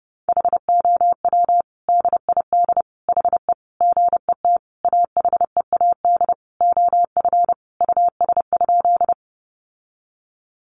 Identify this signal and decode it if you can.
Morse code, heard as a keyed tone, about 30 wpm, 700 Hz. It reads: HOW DID HE GET AHEAD OF US?
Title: Sentences from Top 1100 Words 30wpm